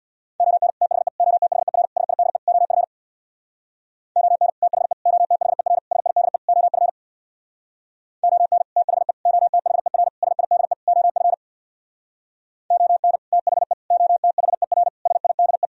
Morse code, heard as a keyed tone, about 61 wpm, 700 Hz. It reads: ON THE OTHER SIDE OF ON THE OTHER SIDE OF ON THE OTHER SIDE OF ON THE OTHER SIDE
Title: Speed-Racing - Sets of 5 Words 50wpm